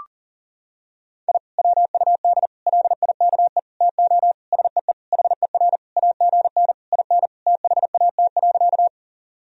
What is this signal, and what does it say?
Morse code, heard as a keyed tone, about 37 wpm, 700 Hz. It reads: I WUD LIKE TO SEE HER AGN IN THAT.
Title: Sentences from Top 200 Words and Abbreviations 40wpm